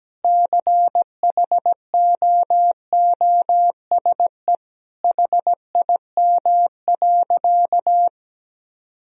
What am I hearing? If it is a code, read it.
Morse code, heard as a keyed tone, about 17 wpm, 700 Hz. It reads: CHOOSE HIM.
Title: Sentences from Top 1000 Words - Review Part 4 - 17wpm